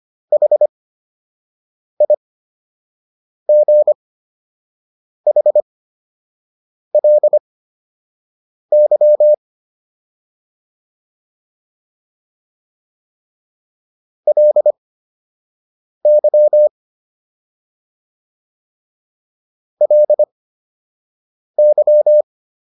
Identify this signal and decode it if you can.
Morse code, heard as a keyed tone, about 25 wpm, spaced as for 6 wpm, 600 Hz. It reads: HIGHLY LY LY